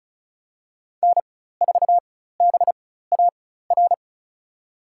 Morse code, heard as a keyed tone, about 36 wpm, spaced as for 16 wpm, 700 Hz. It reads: N4BAR
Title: USA Call Signs - 5 Characters - 4x Character Spacing - 35wpm